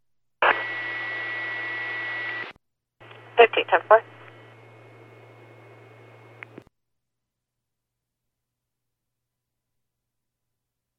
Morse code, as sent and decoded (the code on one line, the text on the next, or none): none